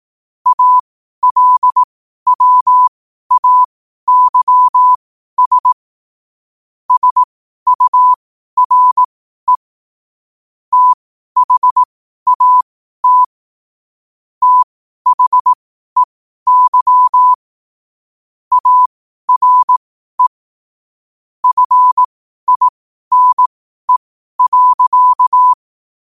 Morse code, heard as a keyed tone, about 18 wpm, spaced as for 12 wpm, 1000 Hz. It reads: ALWAYS SURE THAT THEY ARE FINE.
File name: qotdfiles / 12wpm